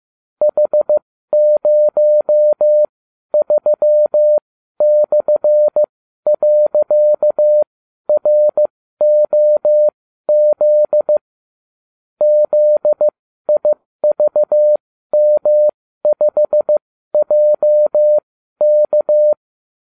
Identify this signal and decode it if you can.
Morse code, heard as a keyed tone, about 15 wpm, spaced as for 12 wpm, 600 Hz. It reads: H03/.ROZ ZIVM5JK